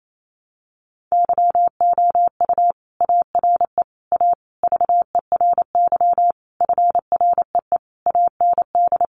Morse code, heard as a keyed tone, about 28 wpm, 700 Hz. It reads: YOU ARE A VERY FREE AND
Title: Sentences from Top 1000 Words 28wpm